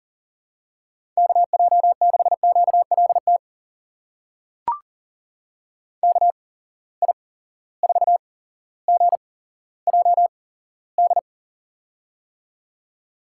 Morse code, heard as a keyed tone, about 41 wpm, spaced as for 17 wpm, 700 Hz. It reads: KJ6QLT KI4GJD
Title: USA Call Signs - 6 Characters - 8x Character Spacing - 40wpm